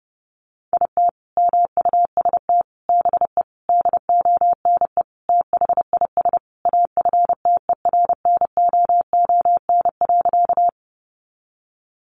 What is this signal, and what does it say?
Morse code, heard as a keyed tone, about 30 wpm, 700 Hz. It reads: IT MUST BE DONE THIS AFTERNOON.
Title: Sentences from Top 1500 Words 30wpm